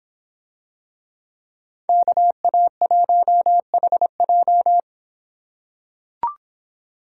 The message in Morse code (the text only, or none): KA1HJ